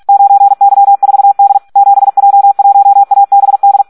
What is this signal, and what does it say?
Morse code, heard as a keyed tone, about 46 wpm, 800 Hz. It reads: 9Y4D 7J1ABD
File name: Audio practica CW / SD 1 46